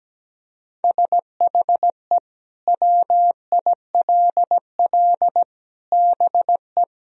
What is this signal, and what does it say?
Morse code, heard as a keyed tone, about 17 wpm, 700 Hz. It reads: SHE WILL BE
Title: Sentences from Top 1000 Words - Review Part 1 - 17wpm